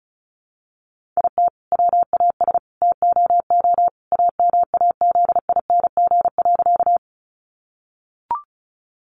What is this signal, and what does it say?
Morse code, heard as a keyed tone, about 34 wpm, 700 Hz. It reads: IT WAS TOO AMAZING.
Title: Sentences from Top 1500 Words 35wpm